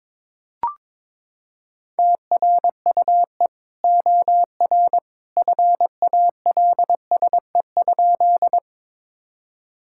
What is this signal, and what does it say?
Morse code, heard as a keyed tone, about 22 wpm, 700 Hz. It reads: TRUE OR FALSE?
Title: Sentences from Top 1600 Words 22wpm